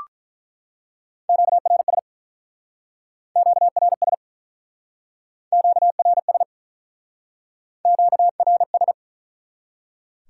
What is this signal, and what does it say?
Morse code, heard as a keyed tone, about 44 wpm, 700 Hz. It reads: QRS QRS QRS QRS
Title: Speed-Racing Top 180 QSO Elements 35wpm